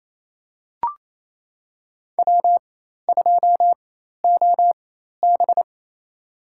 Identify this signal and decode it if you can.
Morse code, heard as a keyed tone, about 27 wpm, spaced as for 13 wpm, 700 Hz. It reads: W2OB
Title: USA Call Signs - 4 Characters - 4x Character Spacing - 28wpm